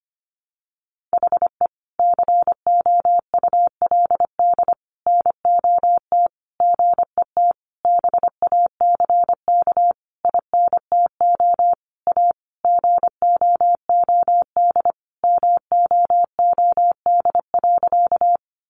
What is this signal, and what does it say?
Morse code, heard as a keyed tone, about 25 wpm, 700 Hz. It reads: HE COULD NOT GET BACK INTO A GOOD MOOD.
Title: Sentences from Top 1100 Words 25wpm